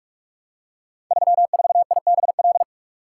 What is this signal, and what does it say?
Morse code, heard as a keyed tone, about 44 wpm, 700 Hz. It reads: 34IBL
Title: Code Groups - 5 Letter:Numbers - 45wpm